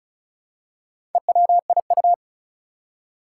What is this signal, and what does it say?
Morse code, heard as a keyed tone, about 36 wpm, 700 Hz. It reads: EWIU
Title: Four Challenging Letter-Number Combinations 35wpm